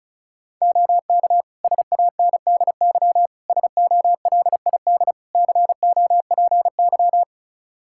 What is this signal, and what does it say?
Morse code, heard as a keyed tone, about 36 wpm, 700 Hz. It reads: OK SANDY SOLID COPY
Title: Practice QSO - Exchange 3 - More Info - 35wpm